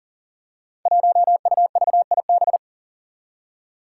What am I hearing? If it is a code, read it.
Morse code, heard as a keyed tone, about 39 wpm, 700 Hz. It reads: 1UVIB
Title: Five Challenging Letter-Number Combinations 40wpm